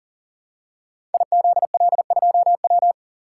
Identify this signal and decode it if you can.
Morse code, heard as a keyed tone, about 38 wpm, 700 Hz. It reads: IZL2W
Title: Code Groups - 5 Letter:Numbers - 40wpm